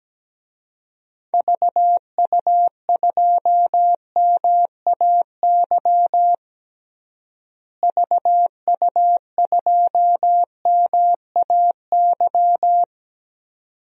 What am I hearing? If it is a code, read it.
Morse code, heard as a keyed tone, about 17 wpm, 700 Hz. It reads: VU2MAY VU2MAY
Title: DXCC-100 Call Signs - 6 Characters - Repeated 2x - 17wpm